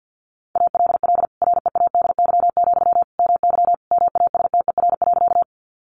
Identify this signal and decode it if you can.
Morse code, heard as a keyed tone, about 50 wpm, 700 Hz. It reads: ALL READY, MY MASTER.
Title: Sentences from Top 1300 Words 50wpm